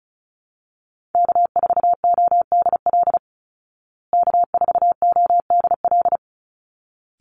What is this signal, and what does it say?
Morse code, heard as a keyed tone, about 34 wpm, 700 Hz. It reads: K4ODL K4ODL
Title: Call Signs Repeated 3x - 35wpm